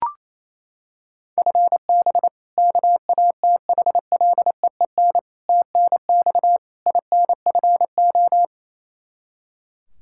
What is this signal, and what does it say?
Morse code, heard as a keyed tone, about 28 wpm, 700 Hz. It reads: FB KATHLEEN TNX INFO
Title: Practice QSO - Exchange 3 - More Info - 28wpm